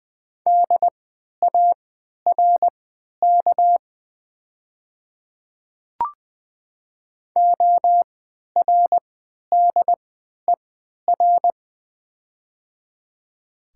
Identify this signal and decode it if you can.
Morse code, heard as a keyed tone, about 20 wpm, spaced as for 11 wpm, 700 Hz. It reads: DARK ORDER
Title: 3x Character Spacing - Top 500 Words - 20wpm